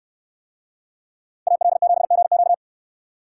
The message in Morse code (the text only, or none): AL7GQ